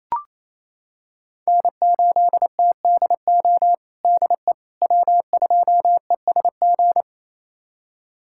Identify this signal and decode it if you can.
Morse code, heard as a keyed tone, about 27 wpm, 700 Hz. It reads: N8TDO DE W2ESG
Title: USA Callsign DE Callsign - 5 Characters - 28wpm